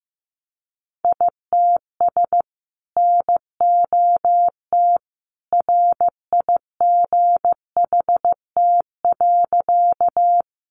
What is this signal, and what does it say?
Morse code, heard as a keyed tone, about 15 wpm, 700 Hz. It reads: ITS NOT RIGHT.
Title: Sentences from Top 600 Words 15wpm